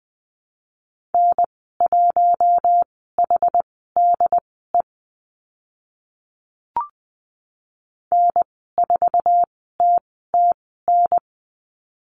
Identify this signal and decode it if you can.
Morse code, heard as a keyed tone, about 20 wpm, spaced as for 14 wpm, 700 Hz. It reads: N1HDE N4TTN
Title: 076 Lesson 13g - Letter L - Call Signs 20wpm